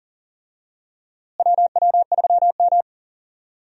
Morse code, heard as a keyed tone, about 40 wpm, 700 Hz. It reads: WW3M
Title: USA Call Signs - 4 Characters - Repeated 2x - 40wpm